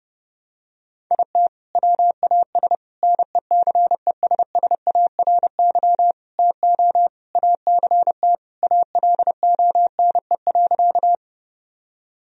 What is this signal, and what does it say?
Morse code, heard as a keyed tone, about 32 wpm, 700 Hz. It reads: IT WAS NECESSARY TO ACT ALONE.
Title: Sentences from Top 1000 Words - Review Part 1 - 30wpm